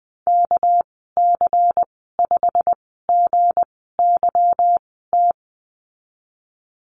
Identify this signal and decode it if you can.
Morse code, heard as a keyed tone, about 20 wpm, spaced as for 14 wpm, 700 Hz. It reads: KC5GYT
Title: 219 Lesson 35g - Number 0 - Call Signs 20wpm